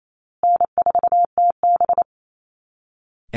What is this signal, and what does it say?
Morse code, heard as a keyed tone, about 28 wpm, 700 Hz. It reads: N4TB
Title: Easy Call Signs Repeated 3x - 28wpm